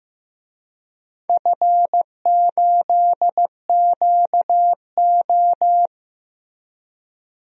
Morse code, heard as a keyed tone, about 15 wpm, 700 Hz. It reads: F8QO
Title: DXCC-100 Call Signs - 4 Characters - Repeated 3x - 15wpm